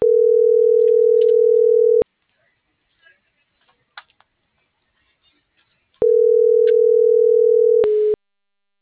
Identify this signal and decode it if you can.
Ringback tone (North America), 440 + 480 Hz.